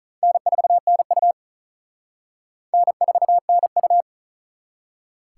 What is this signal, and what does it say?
Morse code, heard as a keyed tone, about 43 wpm, 700 Hz. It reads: N4NU N4NU
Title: Speed Racing - USA 1x2 Call Signs 35wpm